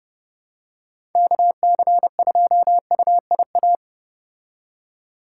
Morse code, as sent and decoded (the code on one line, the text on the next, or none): -.- -.-. ..--- ..- .. .-
KC2UIA